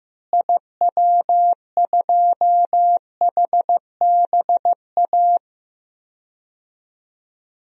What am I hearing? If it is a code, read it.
Morse code, heard as a keyed tone, about 15 wpm, 700 Hz. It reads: IW2HBA
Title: DXCC-100 Call Signs - 6 Characters - Repeated 3x - 15wpm